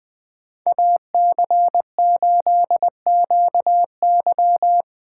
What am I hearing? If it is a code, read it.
Morse code, heard as a keyed tone, about 20 wpm, 700 Hz. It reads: AC8QY
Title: USA Call Signs - 5 Characters - 6x Character Spacing - 20wpm